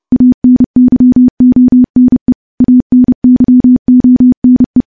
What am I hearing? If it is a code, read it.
Morse code, heard as a keyed tone, about 28 wpm, 250 Hz. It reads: ANYONE ANYONE
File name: morse / morse-training-30WPM